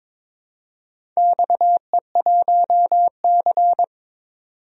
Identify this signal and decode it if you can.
Morse code, heard as a keyed tone, about 22 wpm, 700 Hz. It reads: XE1C
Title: DXCC-100 Call Signs - 4 Characters - 22wpm